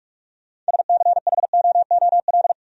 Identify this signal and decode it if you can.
Morse code, heard as a keyed tone, about 58 wpm, 700 Hz. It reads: SCHOOL